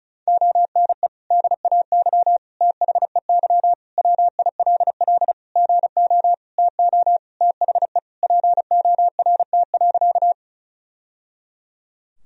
Morse code, heard as a keyed tone, about 36 wpm, 700 Hz. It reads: ONE DAY THEY WILL GO TO THE PORT.
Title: Sentences from Top 1000 Words - Review Part 1 - 35wpm